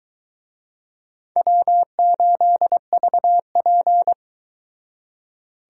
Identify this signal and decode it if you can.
Morse code, heard as a keyed tone, about 23 wpm, 700 Hz. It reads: W8VP